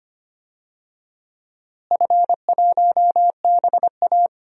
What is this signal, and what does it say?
Morse code, heard as a keyed tone, about 25 wpm, 700 Hz. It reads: F1BA
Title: DXCC-100 Call Signs - 4 Characters - Repeated 2x - 25wpm